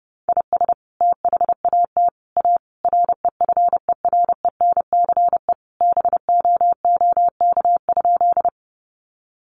Morse code, heard as a keyed tone, about 28 wpm, 700 Hz. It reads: IS THAT A REFERENCE BOOK?
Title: Sentences from Top 1200 Words 30wpm